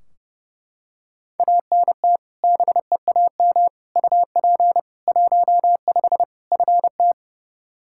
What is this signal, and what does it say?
Morse code, heard as a keyed tone, about 29 wpm, 700 Hz. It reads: ANT BEAM UP 15 FT
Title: Practice QSO - Exchange 2 - Info - 30wpm